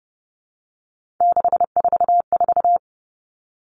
Morse code, heard as a keyed tone, about 29 wpm, 700 Hz. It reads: 644
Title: Numbers - 3-Digit 30wpm